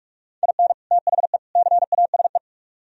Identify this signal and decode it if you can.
Morse code, heard as a keyed tone, about 46 wpm, 700 Hz. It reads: IN THE CASE